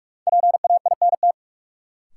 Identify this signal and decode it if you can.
Morse code, heard as a keyed tone, about 46 wpm, 700 Hz. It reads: PAINT
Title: Sentences from Top 1000 Words - Review Part 6 - New Word Drill - 45wpm